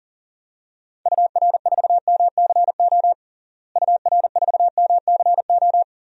Morse code, heard as a keyed tone, about 40 wpm, 700 Hz. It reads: UR4MCO UR4MCO
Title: DXCC-100 Call Signs - 6 Characters - Repeated 2x - 40wpm